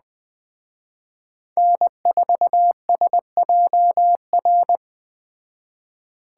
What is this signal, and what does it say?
Morse code, heard as a keyed tone, about 20 wpm, 700 Hz. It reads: N4SJR